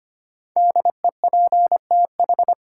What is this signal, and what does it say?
Morse code, heard as a keyed tone, about 25 wpm, 700 Hz. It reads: DEPTH
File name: 5 Letter Words - First 3 Letters Given - 25wpm